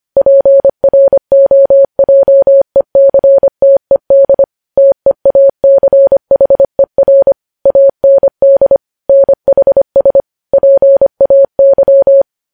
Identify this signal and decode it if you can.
Morse code, heard as a keyed tone, about 25 wpm, 550 Hz. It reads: PROJECTED TEACHER AND NHS PAY
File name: podcast / News headlines-1x-25x25